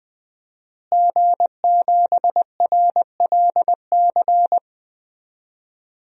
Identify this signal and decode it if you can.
Morse code, heard as a keyed tone, about 20 wpm, 700 Hz. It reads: G7RLC
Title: DXCC-100 Call Signs - 5 Characters - Repeated 2x - 20wpm